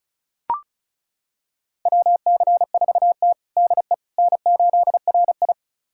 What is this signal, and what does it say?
Morse code, heard as a keyed tone, about 36 wpm, 700 Hz. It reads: WC4T DE N8RI
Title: USA Callsign DE Callsign - 4 Characters - 35wpm